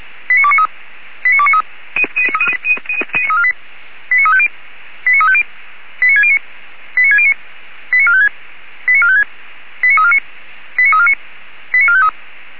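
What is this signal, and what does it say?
Five-tone selective call (ZVEI), heard as quick runs of five tones, tones of about 70 ms. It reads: 89293 89293 89350 89350 89609 89609 89456 89456 89349 89349 89453
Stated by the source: ZVEI five-tone call received on SDR.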